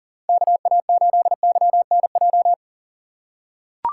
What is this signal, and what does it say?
Morse code, heard as a keyed tone, about 37 wpm, 700 Hz. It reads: KA8YNJ
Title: USA Call Signs - 6 Characters - 2x Character Spacing - 40wpm